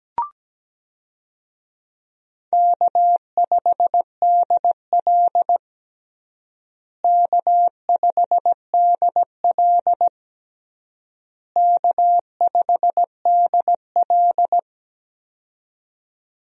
Morse code, heard as a keyed tone, about 17 wpm, 700 Hz. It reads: K5DL K5DL K5DL
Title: USA Call Signs - 4 Characters - Repeated 3x - 17wpm